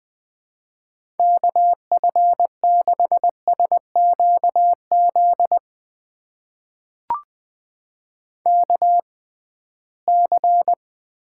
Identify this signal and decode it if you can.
Morse code, heard as a keyed tone, about 21 wpm, spaced as for 15 wpm, 700 Hz. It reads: KF6SQZ KC
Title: USA Call Signs - 6 Characters - 6x Character Spacing - 20wpm